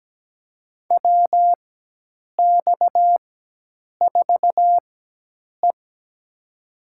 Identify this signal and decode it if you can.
Morse code, heard as a keyed tone, about 17 wpm, spaced as for 8 wpm, 700 Hz. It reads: WX4E
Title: USA Call Signs - 4 Characters - 4x Character Spacing - 17wpm